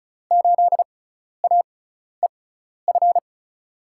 Morse code, heard as a keyed tone, about 36 wpm, spaced as for 12 wpm, 700 Hz. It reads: 8AEF
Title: USA Call Signs - 6 Characters - 6x Character Spacing - 35wpm